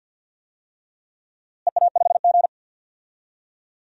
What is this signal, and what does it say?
Morse code, heard as a keyed tone, about 49 wpm, 700 Hz. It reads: EA5G
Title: DXCC-100 Call Signs - 4 Characters - Repeated 3x - 50wpm